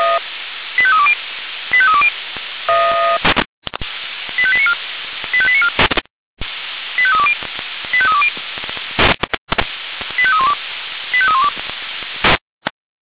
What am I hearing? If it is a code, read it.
Five-tone selective call (ZVEI), heard as quick runs of five tones, tones of about 70 ms. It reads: 85320 85320 85894 85894 85320 85320 85312 85312
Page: SDR/ZVEI/85312